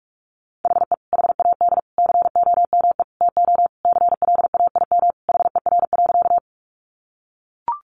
Morse code, heard as a keyed tone, about 44 wpm, 700 Hz. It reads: HE HAD COME TO CLAIM HER.
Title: Sentences from Top 1500 Words 45wpm